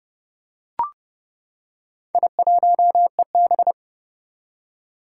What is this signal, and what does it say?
Morse code, heard as a keyed tone, about 31 wpm, 700 Hz. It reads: I1EB
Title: Four Challenging Letter-Number Combinations 30wpm